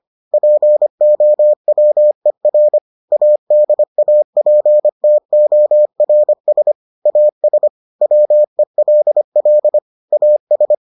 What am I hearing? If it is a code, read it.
Morse code, heard as a keyed tone, about 25 wpm, 600 Hz. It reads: POWER ADAPTORS AS WELL AS